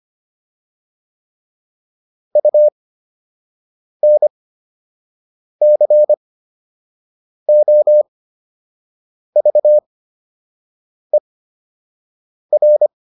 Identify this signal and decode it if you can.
Morse code, heard as a keyed tone, about 25 wpm, spaced as for 6 wpm, 600 Hz. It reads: UNCOVER
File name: UN7 06